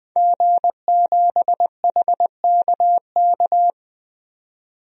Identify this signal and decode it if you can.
Morse code, heard as a keyed tone, about 20 wpm, 700 Hz. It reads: G7HKK
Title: DXCC-100 Call Signs - 5 Characters - 20wpm